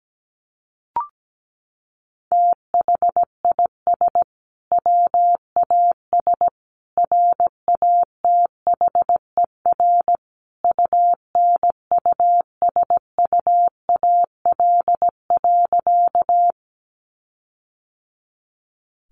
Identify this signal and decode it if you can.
Morse code, heard as a keyed tone, about 17 wpm, 700 Hz. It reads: THIS WAS RATHER UNUSUAL.
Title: Sentences from Top 1400 Words 17wpm